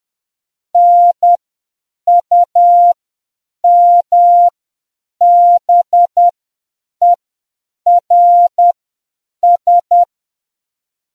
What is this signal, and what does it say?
Morse code, heard as a keyed tone, about 10 wpm, spaced as for 7 wpm, 700 Hz. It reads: NUMBERS